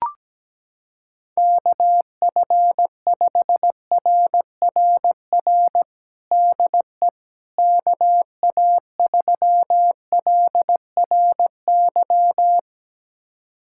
Morse code, heard as a keyed tone, about 17 wpm, 700 Hz. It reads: KF5RRR DE KA3LRY
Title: USA Callsign DE Callsign - 6 Characters - 17wpm